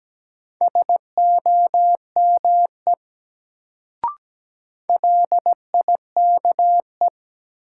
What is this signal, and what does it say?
Morse code, heard as a keyed tone, about 17 wpm, 700 Hz. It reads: SOME LIKE
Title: Sentences from Top 100 Words - New Word Drill - 17wpm